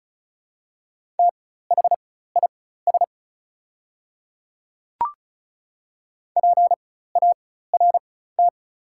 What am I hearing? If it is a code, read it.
Morse code, heard as a keyed tone, about 34 wpm, spaced as for 16 wpm, 700 Hz. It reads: THIS PART
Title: 4x Character Spacing - Top 200 Words - 35wpm